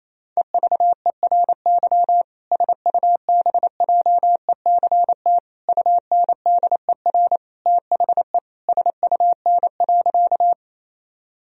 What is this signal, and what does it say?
Morse code, heard as a keyed tone, about 28 wpm, 700 Hz. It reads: EVERY SUBJECT UNDER THE SUN.
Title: Sentences from Top 1000 Words - Review Part 1 - 28wpm